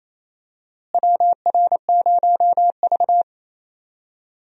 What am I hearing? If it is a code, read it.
Morse code, heard as a keyed tone, about 28 wpm, 700 Hz. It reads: WR0V
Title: USA Call Signs - 4 Characters - Repeated 2x - 28wpm